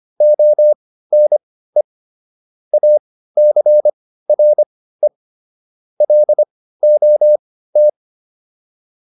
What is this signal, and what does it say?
Morse code, heard as a keyed tone, about 25 wpm, spaced as for 15 wpm, 600 Hz. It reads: ONE ACRE LOT
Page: intermediate-practice-files